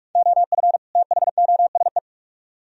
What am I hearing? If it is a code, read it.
Morse code, heard as a keyed tone, about 45 wpm, 700 Hz. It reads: OF THOSE